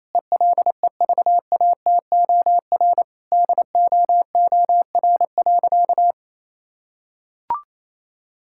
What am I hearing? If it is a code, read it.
Morse code, heard as a keyed tone, about 28 wpm, 700 Hz. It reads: ELEVATOR DOOR.
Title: Sentences from Top 1600 Words 28wpm